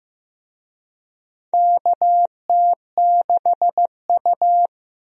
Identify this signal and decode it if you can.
Morse code, heard as a keyed tone, about 15 wpm, 700 Hz. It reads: KT6U
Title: USA Call Signs - 4 Characters - Repeated 2x - 15wpm